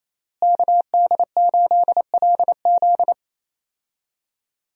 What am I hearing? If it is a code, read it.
Morse code, heard as a keyed tone, about 28 wpm, 700 Hz. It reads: KD8LZ
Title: USA Call Signs - 5 Characters - Repeated 3x - 28wpm